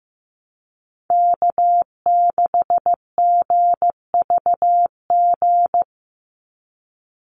Morse code, heard as a keyed tone, about 15 wpm, 700 Hz. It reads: K6GVG
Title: Call Signs Repeated 3x - 15wpm